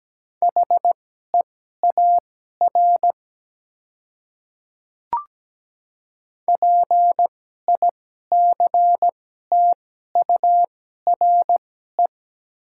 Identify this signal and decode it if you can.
Morse code, heard as a keyed tone, about 17 wpm, spaced as for 12 wpm, 700 Hz. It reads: HEAR PICTURE
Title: 2x Character Spacing - Top 300 Words - 17wpm